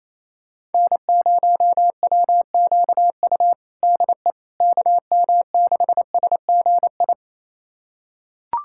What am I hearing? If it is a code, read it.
Morse code, heard as a keyed tone, about 27 wpm, 700 Hz. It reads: N0WQU DE KM6SGI
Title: USA Callsign DE Callsign 28wpm